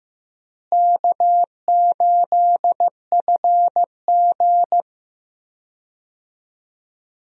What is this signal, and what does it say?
Morse code, heard as a keyed tone, about 15 wpm, 700 Hz. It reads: K8FG
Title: USA Call Signs - 4 Characters - Repeated 3x - 15wpm